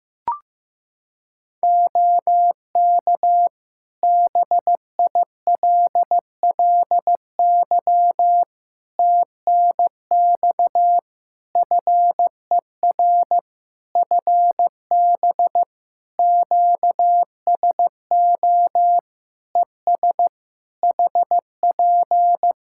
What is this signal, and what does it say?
Morse code, heard as a keyed tone, about 15 wpm, 700 Hz. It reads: OK BILLY TNX FER FB QSO ES HP
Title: Practice QSO - Exchange 4 - Ending - 15wpm